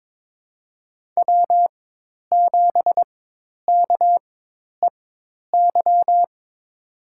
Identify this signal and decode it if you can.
Morse code, heard as a keyed tone, about 22 wpm, spaced as for 10 wpm, 700 Hz. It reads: W7KEY
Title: USA Call Signs - 4x Character Spacing - 22wpm